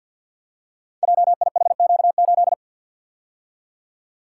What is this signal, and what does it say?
Morse code, heard as a keyed tone, about 53 wpm, 700 Hz. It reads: JIHQ8
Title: Code Groups - 5 Letter:Numbers - 50wpm